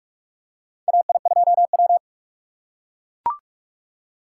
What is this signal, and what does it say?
Morse code, heard as a keyed tone, about 44 wpm, 700 Hz. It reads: AI2W
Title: USA Call Signs - 4 Characters - 3x Character Spacing - 45wpm